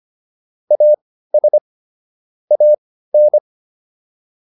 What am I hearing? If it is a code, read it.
Morse code, heard as a keyed tone, about 25 wpm, spaced as for 15 wpm, 600 Hz. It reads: AS AN